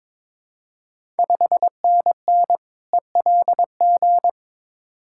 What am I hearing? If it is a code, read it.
Morse code, heard as a keyed tone, about 22 wpm, 690 Hz. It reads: 5NN ELG
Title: Ontario QSO Party Exchange 22wpm